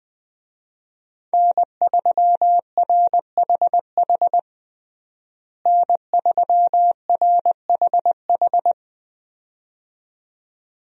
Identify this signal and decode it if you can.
Morse code, heard as a keyed tone, about 20 wpm, 700 Hz. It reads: N3RHH N3RHH
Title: USA Call Signs - 5 Characters - Repeated 2x - 20wpm